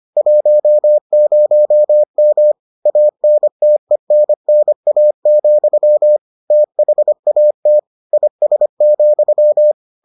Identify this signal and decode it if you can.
Morse code, heard as a keyed tone, about 25 wpm, 600 Hz. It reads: 10M ANTENNA, THAT IS,